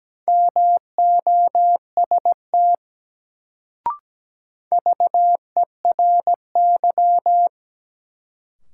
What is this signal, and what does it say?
Morse code, heard as a keyed tone, about 17 wpm, 700 Hz. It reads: MOST VERY
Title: Sentences from Top 200 Words - New Word Drill - 17wpm